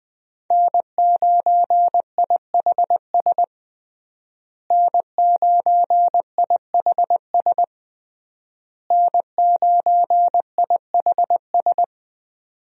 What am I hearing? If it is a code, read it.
Morse code, heard as a keyed tone, about 20 wpm, 700 Hz. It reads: N9IHS N9IHS N9IHS
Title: USA Call Signs - 5 Characters - Repeated 3x - 20wpm